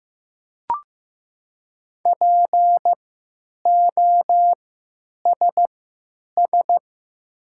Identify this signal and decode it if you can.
Morse code, heard as a keyed tone, about 15 wpm, spaced as for 9 wpm, 700 Hz. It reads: POSS
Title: 3x Character Spacing - Top 500 Words - 15wpm